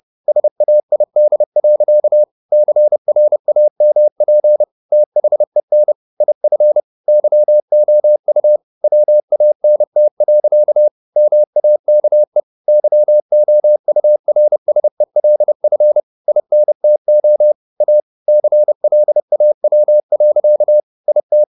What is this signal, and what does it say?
Morse code, heard as a keyed tone, about 30 wpm, 600 Hz. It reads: SAID. CRAMP THEN IF YOU WANT. MAKE YOURSELF INTO A CLAW. IT
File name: the old man and the sea / the old man and the sea 1 30wpm